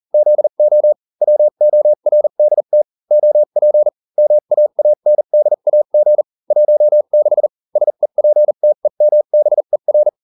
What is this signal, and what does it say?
Morse code, heard as a keyed tone, about 40 wpm, spaced as for 35 wpm, 600 Hz. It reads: ZO WORDT OP MAANDAG 16 SEPTEMBER